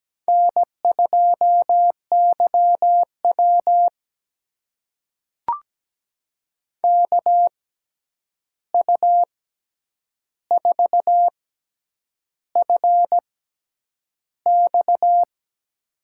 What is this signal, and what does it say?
Morse code, heard as a keyed tone, about 17 wpm, spaced as for 8 wpm, 700 Hz. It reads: N2YW KU4FX
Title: USA Call Signs - 6x Character Spacing - 17wpm